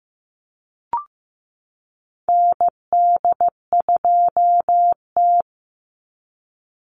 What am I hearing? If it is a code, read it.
Morse code, heard as a keyed tone, about 15 wpm, 700 Hz. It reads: ND2T
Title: Call Signs Repeated 2x - 15wpm